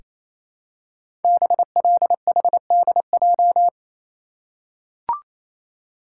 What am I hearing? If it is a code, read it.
Morse code, heard as a keyed tone, about 27 wpm, 700 Hz. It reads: BLHDJ
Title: Code Groups - 5 Letters - 28wpm